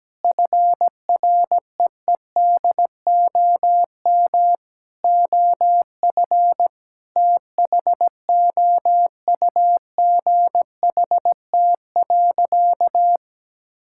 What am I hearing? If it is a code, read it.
Morse code, heard as a keyed tone, about 17 wpm, 700 Hz. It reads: FREEDOM OF THOUGHT.
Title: Sentences from Top 1000 Words - Review Part 5 - 17wpm